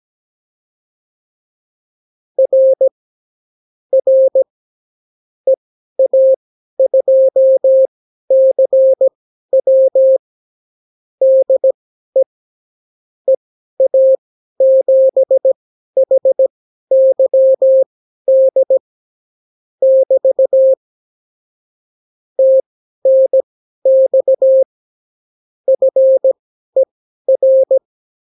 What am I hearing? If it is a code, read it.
Morse code, heard as a keyed tone, about 17 wpm, spaced as for 12 wpm, 530 Hz. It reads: R R EA2CW DE EA7HYD <BT> TNX FER
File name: CW QSO libroCW4 18-8ppm